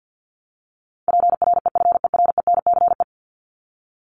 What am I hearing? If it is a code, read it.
Morse code, heard as a keyed tone, about 50 wpm, 700 Hz. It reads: PREFERENCE
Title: Top 1251-1500 Words 50wpm